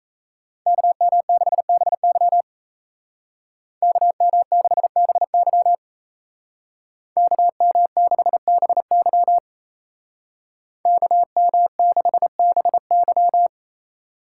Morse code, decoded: KM6BY KM6BY KM6BY KM6BY

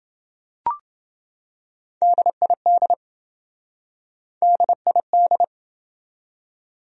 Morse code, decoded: DID DID

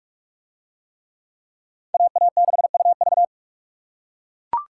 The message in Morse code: AA6UV

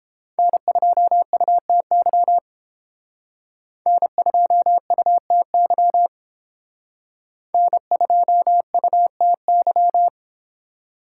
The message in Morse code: N2UTY N2UTY N2UTY

